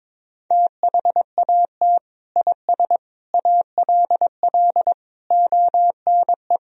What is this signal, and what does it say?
Morse code, heard as a keyed tone, about 22 wpm, 700 Hz. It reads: THAT IS ALL ONE
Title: Sentences from Top 200 Words and Abbreviations 22wpm